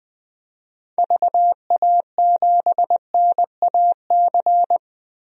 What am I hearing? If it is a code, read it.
Morse code, heard as a keyed tone, about 20 wpm, 700 Hz. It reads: VA7NAC
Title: DXCC-100 Call Signs - 6 Characters - 20wpm